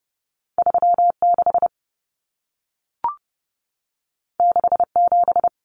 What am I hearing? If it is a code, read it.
Morse code, heard as a keyed tone, about 30 wpm, 700 Hz. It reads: 36 67
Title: Numbers - 2-Digit 30wpm